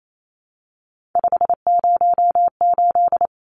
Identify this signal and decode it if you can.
Morse code, heard as a keyed tone, about 28 wpm, 700 Hz. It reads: 508
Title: Numbers - 3-Digit 28wpm